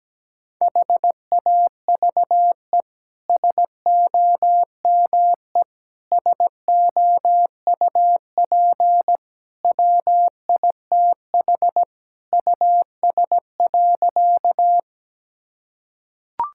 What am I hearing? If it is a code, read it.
Morse code, heard as a keyed tone, about 17 wpm, 700 Hz. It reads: HAVE SOME SOUP WITH US.
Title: Sentences from Top 1000 Words - Review Part 6 - 17wpm